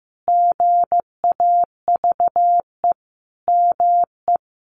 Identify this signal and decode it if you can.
Morse code, heard as a keyed tone, about 14 wpm, 700 Hz. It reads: GAVE ME
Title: Sentences from Top 1500 Words 15wpm